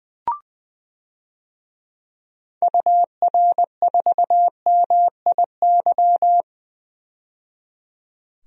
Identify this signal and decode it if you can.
Morse code, heard as a keyed tone, about 20 wpm, 700 Hz. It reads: UR4MIY
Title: DXCC-100 Call Signs - 6 Characters - 20wpm